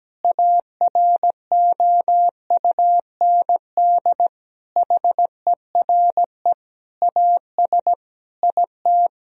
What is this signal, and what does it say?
Morse code, heard as a keyed tone, about 17 wpm, 700 Hz. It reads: AROUND HERE AS IT
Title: Sentences from Top 1000 Words - Review Part 3 - 17wpm